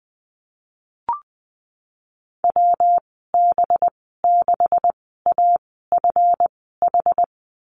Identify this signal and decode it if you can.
Morse code, heard as a keyed tone, about 20 wpm, spaced as for 15 wpm, 700 Hz. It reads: WB6AFH
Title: 193 Lesson 31g - Letter v - Call Signs 20wpm